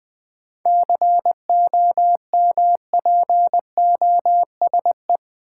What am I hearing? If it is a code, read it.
Morse code, heard as a keyed tone, about 20 wpm, 700 Hz. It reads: COMPOSE
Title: 7 Letter Words - First 3 Letters Given - 20wpm